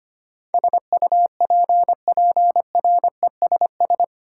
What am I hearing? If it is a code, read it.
Morse code, heard as a keyed tone, about 25 wpm, 700 Hz. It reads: SUPPRESS
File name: 8 Letter Words - First 3 Letters Given - 25wpm